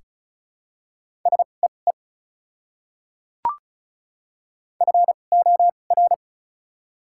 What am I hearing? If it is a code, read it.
Morse code, heard as a keyed tone, about 36 wpm, spaced as for 25 wpm, 700 Hz. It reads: SEE FOR
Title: 2x Character Spacing - Top 100 Words - 35wpm